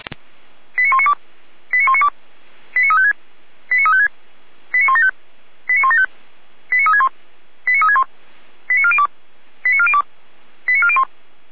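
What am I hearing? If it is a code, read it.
Five-tone selective call (ZVEI), heard as quick runs of five tones, tones of about 70 ms. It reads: 89192 89192 89356 89356 89175 89175 89361 89361 89402 89402 89401
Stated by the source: ZVEI five-tone selective call 89500, received on an SDR, from the Feuerwehr Orlamünde fire brigade.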